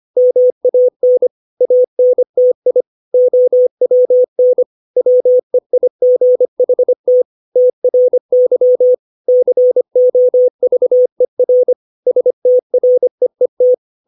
Morse code, heard as a keyed tone, about 25 wpm, 500 Hz. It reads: MAN ANTI OWN WEIGHT TRY COVER STREET
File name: 632 words / 632 Words 25WPM